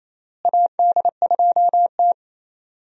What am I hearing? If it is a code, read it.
Morse code, heard as a keyed tone, about 28 wpm, 700 Hz. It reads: AD2T
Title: USA Call Signs - 4 Characters - Repeated 3x - 28wpm